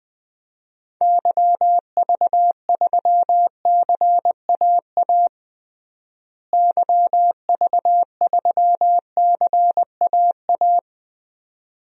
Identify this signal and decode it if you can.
Morse code, heard as a keyed tone, about 20 wpm, 700 Hz. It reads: YV3CAA YV3CAA
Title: DXCC-100 Call Signs - 6 Characters - Repeated 2x - 20wpm